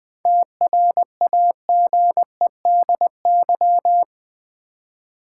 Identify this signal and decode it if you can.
Morse code, heard as a keyed tone, about 20 wpm, 700 Hz. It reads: TRAGEDY
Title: 7 Letter Words - First 3 Letters Given - 20wpm